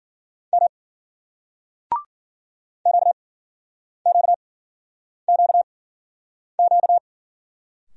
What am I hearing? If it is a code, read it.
Morse code, heard as a keyed tone, about 54 wpm, spaced as for 9 wpm, 700 Hz. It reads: M QQQQ